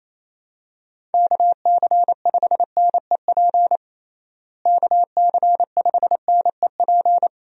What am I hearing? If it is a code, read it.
Morse code, heard as a keyed tone, about 28 wpm, 700 Hz. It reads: KC5NEP KC5NEP
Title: USA Call Signs - 6 Characters - Repeated 2x - 28wpm